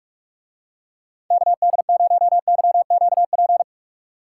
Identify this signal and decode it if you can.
Morse code, heard as a keyed tone, about 45 wpm, 700 Hz. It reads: KD0YQP